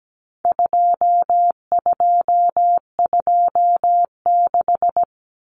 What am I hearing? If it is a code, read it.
Morse code, heard as a keyed tone, about 17 wpm, 700 Hz. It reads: 2226
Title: Common Cut Numbers - 4-Digit 17wpm